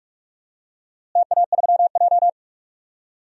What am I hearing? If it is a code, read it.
Morse code, heard as a keyed tone, about 44 wpm, 700 Hz. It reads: TA3J
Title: DXCC-100 Call Signs - 4 Characters - Repeated 3x - 45wpm